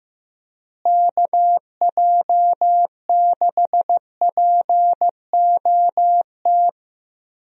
Morse code, heard as a keyed tone, about 15 wpm, 700 Hz. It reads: KJ6POT
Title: USA Call Signs - 6 Characters - Repeated 2x - 15wpm